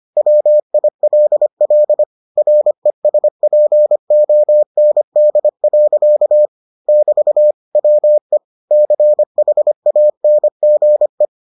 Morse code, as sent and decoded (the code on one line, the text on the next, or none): .-- .. .-.. .-.. / .-. ... .--. --- -. -.. .-.-.- / -...- / .-- . / -.-. .... .- -. --. .
WILL RESPOND. <BT> WE CHANGE